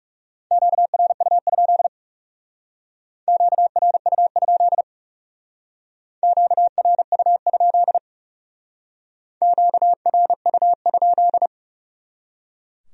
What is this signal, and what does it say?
Morse code, heard as a keyed tone, about 38 wpm, 700 Hz. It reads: QRU? QRU? QRU? QRU?